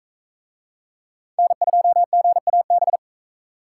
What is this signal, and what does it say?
Morse code, heard as a keyed tone, about 40 wpm, 700 Hz. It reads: N2GAB